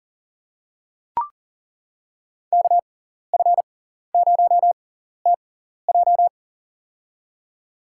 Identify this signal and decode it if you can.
Morse code, heard as a keyed tone, about 40 wpm, spaced as for 14 wpm, 700 Hz. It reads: KF0TJ
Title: USA Call Signs - 5 Characters - 6x Character Spacing - 40wpm